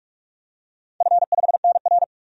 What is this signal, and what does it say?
Morse code, heard as a keyed tone, about 44 wpm, 700 Hz. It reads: F5NR
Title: DXCC-100 Call Signs - 4 Characters - 45wpm